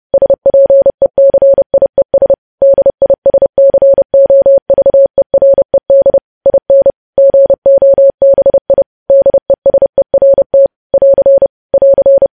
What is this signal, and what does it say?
Morse code, heard as a keyed tone, about 30 wpm, 550 Hz. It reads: SPECIES DISCOVERED IN GOBI DESERT <AR> <AR>
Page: news headlines in morse code at 30 wpm